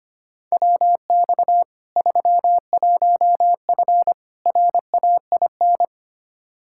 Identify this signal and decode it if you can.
Morse code, heard as a keyed tone, about 25 wpm, 700 Hz. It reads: WX 31F RAIN
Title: Practice QSO - Exchange 2 - Info - 25wpm